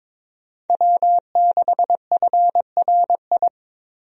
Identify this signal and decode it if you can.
Morse code, heard as a keyed tone, about 22 wpm, 700 Hz. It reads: W6FRI
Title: USA Call Signs - 5 Characters - Repeated 3x - 22wpm